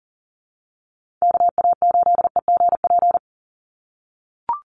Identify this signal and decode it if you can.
Morse code, heard as a keyed tone, about 40 wpm, 700 Hz. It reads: KA8EGP
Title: USA Call Signs 40wpm